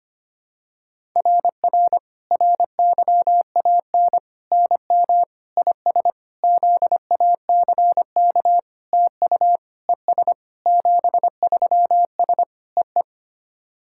Morse code, heard as a keyed tone, about 25 wpm, 700 Hz. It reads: RR RYAN NM IS ZACK TU ES 73S EE
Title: SOTA QSO Practice 25wpm